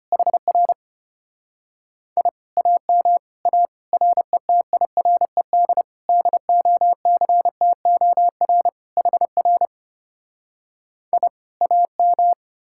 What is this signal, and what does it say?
Morse code, heard as a keyed tone, about 33 wpm, 700 Hz. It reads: HR I AM A RETIRED DOCTOR HR I AM